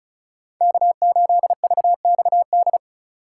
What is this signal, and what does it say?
Morse code, heard as a keyed tone, about 34 wpm, 700 Hz. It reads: K8VXD
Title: USA Call Signs - 5 Characters - 2x Character Spacing - 35wpm